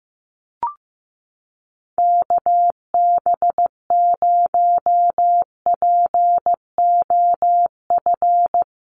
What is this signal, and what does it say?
Morse code, heard as a keyed tone, about 15 wpm, 700 Hz. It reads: KB0POF
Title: USA Call Signs Repeated 2x 15wpm